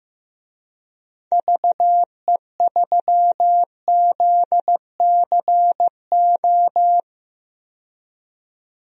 Morse code, heard as a keyed tone, about 15 wpm, 700 Hz. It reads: VE3ZCO
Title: DXCC-100 Call Signs - 6 Characters - 15wpm